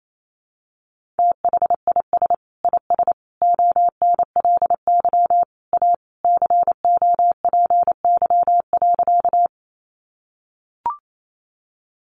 Morse code, heard as a keyed tone, about 28 wpm, 700 Hz. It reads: THIS IS ONLY A COPY.
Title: Sentences from Top 1400 Words 28wpm